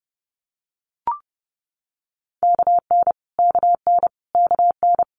Morse code, heard as a keyed tone, about 30 wpm, 690 Hz. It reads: KN KN KN
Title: Multi-letter Phonemes Repeated 3x 30wpm